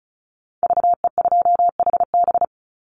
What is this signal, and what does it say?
Morse code, heard as a keyed tone, about 34 wpm, 700 Hz. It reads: VE2HB
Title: Call Signs - 35wpm